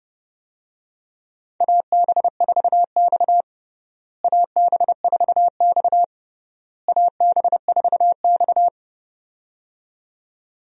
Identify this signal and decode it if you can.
Morse code, heard as a keyed tone, about 29 wpm, 700 Hz. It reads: AB4X AB4X AB4X
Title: USA Call Signs - 4 Characters - Repeated 3x - 30wpm